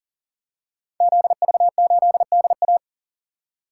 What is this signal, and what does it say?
Morse code, heard as a keyed tone, about 38 wpm, 700 Hz. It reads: ZV8DA
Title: Code Groups - 5 Letter:Numbers - 40wpm